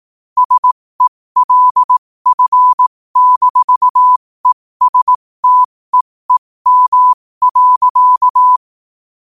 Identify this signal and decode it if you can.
Morse code, heard as a keyed tone, about 18 wpm, spaced as for 15 wpm, 1000 Hz. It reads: SELF-ESTEEM.
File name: qotdfiles / 15wpm